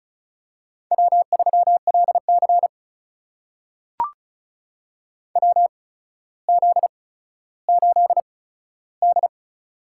Morse code, heard as a keyed tone, about 36 wpm, spaced as for 15 wpm, 700 Hz. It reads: W3LC WZ8D
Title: USA Call Signs - 4 Characters - 8x Character Spacing - 35wpm